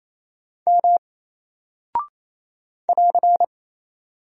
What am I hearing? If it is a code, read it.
Morse code, heard as a keyed tone, about 27 wpm, 700 Hz. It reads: M <AR>
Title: Single Character (Letter:Number:Punctuation:Prosign) 28wpm